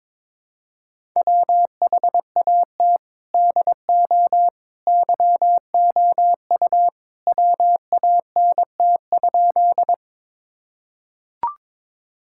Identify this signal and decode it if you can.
Morse code, heard as a keyed tone, about 22 wpm, 700 Hz. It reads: WHAT DO YOU WANT?
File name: practice / 22wpm / Sentences with 2-Word Phrases - 22wpm